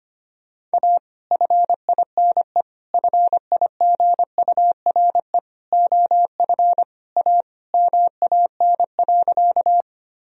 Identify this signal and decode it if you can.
Morse code, heard as a keyed tone, about 25 wpm, 700 Hz. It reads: A FINE FIGURE OF A MAN.
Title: Sentences from Top 1000 Words - Review Part 2 - 25wpm